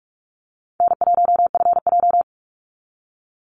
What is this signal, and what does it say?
Morse code, heard as a keyed tone, about 46 wpm, 700 Hz. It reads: N1FJ
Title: NA SOTA Activator Callsigns 45wpm